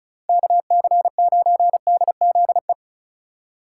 Morse code, heard as a keyed tone, about 43 wpm, 700 Hz. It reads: KC9DZE
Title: Speed Racing - USA Call Signs 35wpm